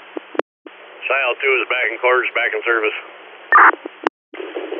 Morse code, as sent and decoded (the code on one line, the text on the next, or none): none